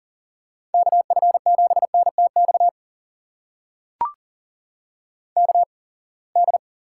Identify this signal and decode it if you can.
Morse code, heard as a keyed tone, about 40 wpm, spaced as for 27 wpm, 700 Hz. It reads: KF7NTX KD
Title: USA Call Signs - 6 Characters - 8x Character Spacing - 40wpm